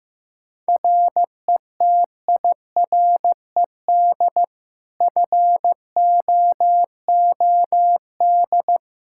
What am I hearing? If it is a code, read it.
Morse code, heard as a keyed tone, about 15 wpm, 700 Hz. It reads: RETIRED FOOD
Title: Practice QSO - Exchange 3 - More Info - 15wpm